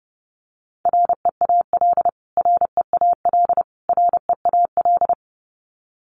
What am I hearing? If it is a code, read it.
Morse code, heard as a keyed tone, about 30 wpm, 700 Hz. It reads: REAL REAL REAL
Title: Top 300 Words Repeated 30wpm